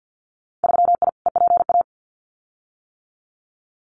Morse code, heard as a keyed tone, about 50 wpm, 700 Hz. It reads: 3I EPA
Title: Winter Field Day Exchanges Encore 50wpm